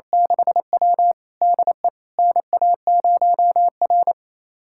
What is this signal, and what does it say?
Morse code, heard as a keyed tone, about 28 wpm, 700 Hz. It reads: 6W DE NA0R
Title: USA Callsign DE Callsign - 4 Characters - 28wpm